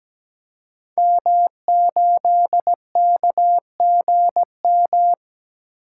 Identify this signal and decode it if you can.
Morse code, heard as a keyed tone, about 17 wpm, 700 Hz. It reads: M8KGM
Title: Code Groups - 5 Letter:Numbers - 17wpm